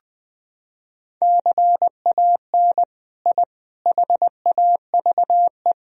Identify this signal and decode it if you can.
Morse code, heard as a keyed tone, about 21 wpm, 700 Hz. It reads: CAN I HAVE
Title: Sentences from Top 1000 Words - Review Part 2 - 20wpm